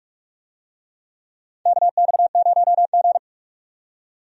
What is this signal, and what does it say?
Morse code, heard as a keyed tone, about 46 wpm, 700 Hz. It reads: KX0G
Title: USA Call Signs - 4 Characters - 45wpm